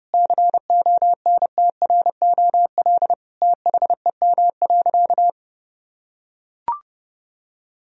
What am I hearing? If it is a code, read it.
Morse code, heard as a keyed tone, about 32 wpm, 700 Hz. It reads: CONTROL THEM.
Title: Sentences from Top 1000 Words - Review Part 2 - 30wpm